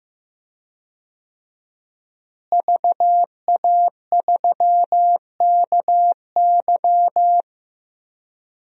Morse code, heard as a keyed tone, about 16 wpm, 700 Hz. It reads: VA3KY
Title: DXCC-100 Call Signs - 5 Characters - Repeated 3x - 15wpm